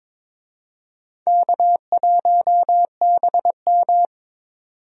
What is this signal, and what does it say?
Morse code, heard as a keyed tone, about 22 wpm, 700 Hz. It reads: K1BM
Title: USA Call Signs - 4 Characters - 22wpm